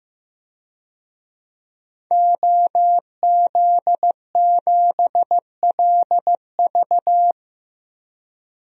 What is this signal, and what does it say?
Morse code, heard as a keyed tone, about 15 wpm, 700 Hz. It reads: OZ7LV
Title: DXCC-100 Call Signs - 5 Characters - 15wpm